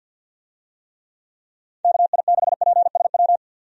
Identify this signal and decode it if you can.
Morse code, heard as a keyed tone, about 50 wpm, 700 Hz. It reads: KI6PSW
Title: USA Call Signs - 6 Characters - 50wpm